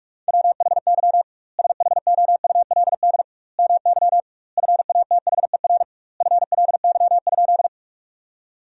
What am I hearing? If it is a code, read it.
Morse code, heard as a keyed tone, about 41 wpm, 700 Hz. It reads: WHY SHOULD MY FATHER FLY?
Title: 2x Word-Spacing - Sentences from Top 500 Words - 45wpm